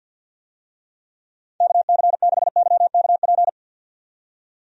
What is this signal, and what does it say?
Morse code, heard as a keyed tone, about 48 wpm, 700 Hz. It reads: KC6YKP
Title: USA Call Signs - 6 Characters - 50wpm